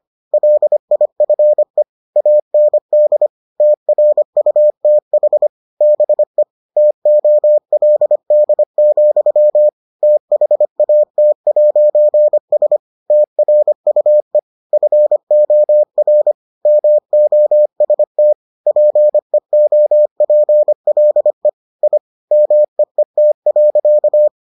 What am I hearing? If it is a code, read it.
Morse code, heard as a keyed tone, about 25 wpm, 600 Hz. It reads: LIFE AND TRUTH BE TOLD, THAT'S TRUE FOR MOST PEOPLE I MEET.